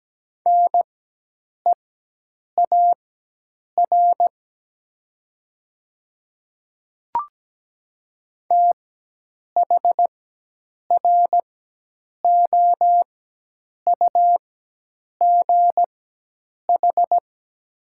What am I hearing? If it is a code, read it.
Morse code, heard as a keyed tone, about 17 wpm, spaced as for 8 wpm, 700 Hz. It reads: NEAR THROUGH
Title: 4x Character Spacing - Top 300 Words - 17wpm